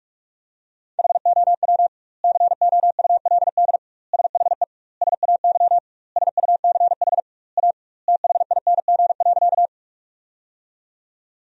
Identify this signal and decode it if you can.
Morse code, heard as a keyed tone, about 46 wpm, 700 Hz. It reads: HOW COULD SHE SAY SUCH A THING.
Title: 2x Word-Spacing - Sentences from Top 200 Words - 45wpm